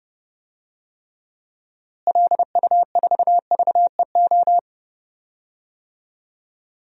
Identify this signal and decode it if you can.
Morse code, heard as a keyed tone, about 30 wpm, 700 Hz. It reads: LU4VEO
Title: DXCC-100 Call Signs - 6 Characters - Repeated 2x - 30wpm